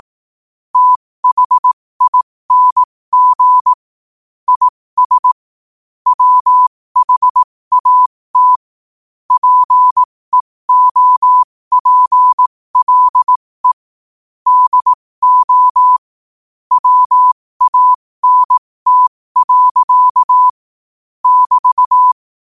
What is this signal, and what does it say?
Morse code, heard as a keyed tone, about 18 wpm, spaced as for 15 wpm, 1000 Hz. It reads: THING IS WHAT PEOPLE DO WANT. <BT>
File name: qotdfiles / 15wpm